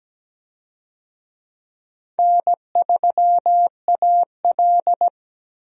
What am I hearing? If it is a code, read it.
Morse code, heard as a keyed tone, about 17 wpm, 700 Hz. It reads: N3AL
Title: USA Call Signs - 4 Characters - 17wpm